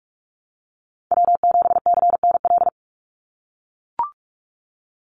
Morse code, heard as a keyed tone, about 44 wpm, 700 Hz. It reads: W7CNL
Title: NA SOTA Chaser Callsigns 45wpm